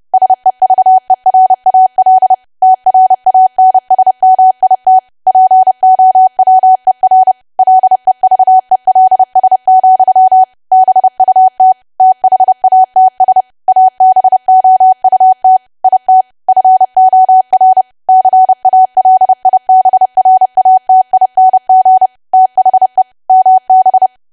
Morse code, heard as a keyed tone, about 30 wpm, 750 Hz. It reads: SEVERAL TRANSMIT POWER LEVELS, BUT THATS ABOUT IT FOR CALIBRATING THE MB
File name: cwa / W1AW-Level3 / 30WPM